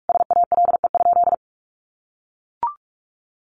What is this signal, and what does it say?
Morse code, heard as a keyed tone, about 44 wpm, 700 Hz. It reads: SALE?
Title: Sentences from Top 1400 Words 45wpm